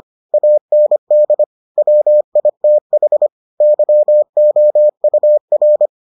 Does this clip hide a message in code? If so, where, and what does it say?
Yes: Morse code from 0.3 to 5.9 s, heard as a keyed tone, about 25 wpm, 600 Hz. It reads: AND WITH YOUR